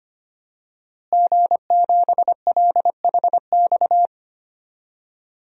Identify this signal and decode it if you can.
Morse code, heard as a keyed tone, about 25 wpm, 700 Hz. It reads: G7LHX
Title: DXCC-100 Call Signs - 5 Characters - Repeated 3x - 25wpm